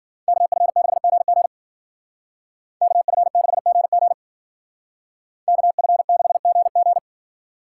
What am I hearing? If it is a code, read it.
Morse code, heard as a keyed tone, about 52 wpm, 700 Hz. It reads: KF6GG KF6GG KF6GG